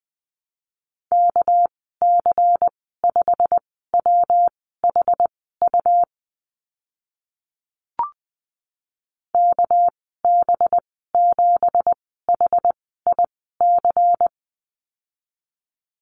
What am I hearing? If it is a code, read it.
Morse code, heard as a keyed tone, about 20 wpm, spaced as for 14 wpm, 700 Hz. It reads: KC5WHU KB7HIC
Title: 213 Lesson 34g - Number 8 - Call Signs 20wpm